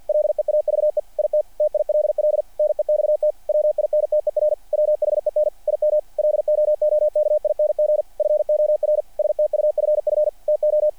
Morse code, heard as a keyed tone, about 48 wpm, 600 Hz. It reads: LEAVE IT TILL NEXT WINTER WHEN IM LOOKING FOR STUFF TO
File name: archived / CRSnet